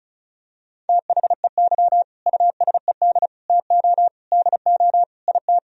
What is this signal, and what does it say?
Morse code, heard as a keyed tone, about 36 wpm, 700 Hz. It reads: THEY USED TO DO IT
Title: Sentences from Top 1000 Words - Review Part 4 - 35wpm